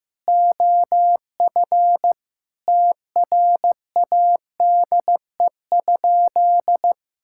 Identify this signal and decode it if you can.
Morse code, heard as a keyed tone, about 15 wpm, 700 Hz. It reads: OF TRADE?
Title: Sentences from Top 1000 Words - Review Part 6 - 15wpm